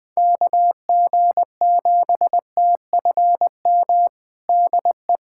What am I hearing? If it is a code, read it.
Morse code, heard as a keyed tone, about 20 wpm, 700 Hz. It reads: KG7TFM DE
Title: USA Callsign DE Callsign - 6 Characters - 20wpm